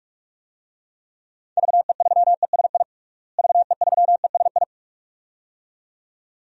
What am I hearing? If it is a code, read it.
Morse code, heard as a keyed tone, about 41 wpm, 700 Hz. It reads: VE3ESI VE3ESI
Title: DXCC-100 Call Signs - 6 Characters - Repeated 2x - 45wpm